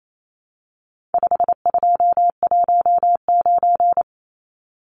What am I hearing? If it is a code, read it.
Morse code, heard as a keyed tone, about 28 wpm, 700 Hz. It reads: 5219
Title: Numbers - 4-Digit 28wpm